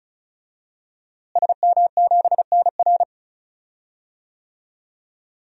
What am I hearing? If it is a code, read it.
Morse code, heard as a keyed tone, about 36 wpm, 700 Hz. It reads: SM7NR
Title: DXCC-100 Call Signs - 5 Characters - Repeated 3x - 35wpm